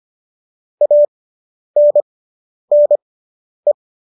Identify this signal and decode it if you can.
Morse code, heard as a keyed tone, about 25 wpm, spaced as for 10 wpm, 590 Hz. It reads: ANNE